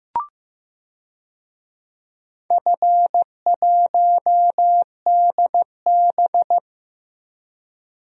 Morse code, heard as a keyed tone, about 15 wpm, 700 Hz. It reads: F1DB
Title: DXCC-100 Call Signs - 4 Characters - Repeated 2x - 15wpm